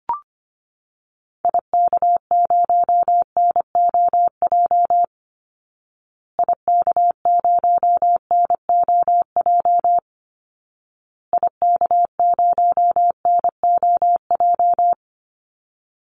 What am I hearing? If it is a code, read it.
Morse code, heard as a keyed tone, about 25 wpm, 700 Hz. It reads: IK0NOJ IK0NOJ IK0NOJ
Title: Call Signs Repeated 2x - 25wpm